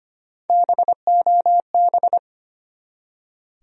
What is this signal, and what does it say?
Morse code, heard as a keyed tone, about 25 wpm, 700 Hz. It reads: BOB
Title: USA Common Names - Max 3 Letters - 25wpm